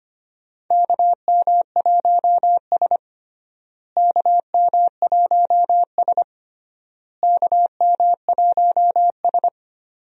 Morse code, heard as a keyed tone, about 25 wpm, 700 Hz. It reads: KM1S KM1S KM1S
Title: USA Call Signs - 4 Characters - Repeated 3x - 25wpm